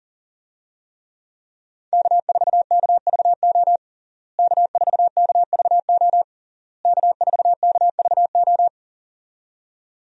Morse code, heard as a keyed tone, about 40 wpm, 700 Hz. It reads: K4KVO K4KVO K4KVO
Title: USA Call Signs - 5 Characters - Repeated 3x - 40wpm